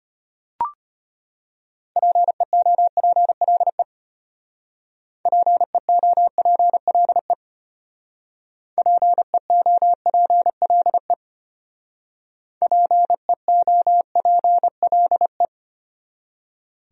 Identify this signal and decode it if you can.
Morse code, heard as a keyed tone, about 32 wpm, 700 Hz. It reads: PEOPLE PEOPLE PEOPLE PEOPLE